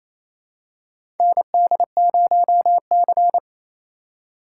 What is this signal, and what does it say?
Morse code, heard as a keyed tone, about 28 wpm, 700 Hz. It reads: ND0C
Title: USA Call Signs - 4 Characters - 2x Character Spacing - 28wpm